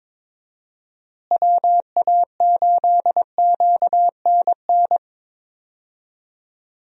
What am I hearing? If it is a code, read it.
Morse code, heard as a keyed tone, about 22 wpm, 700 Hz. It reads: WA8QNN
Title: USA Call Signs - 6 Characters - Repeated 3x - 22wpm